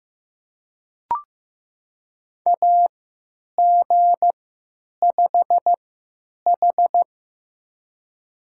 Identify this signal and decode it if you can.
Morse code, heard as a keyed tone, about 15 wpm, spaced as for 9 wpm, 700 Hz. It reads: AG5H
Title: USA Call Signs - 4 Characters - 3x Character Spacing - 15wpm